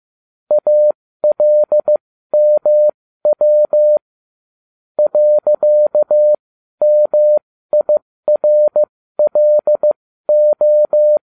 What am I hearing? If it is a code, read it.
Morse code, heard as a keyed tone, about 15 wpm, spaced as for 12 wpm, 600 Hz. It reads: ALMW .MIRLO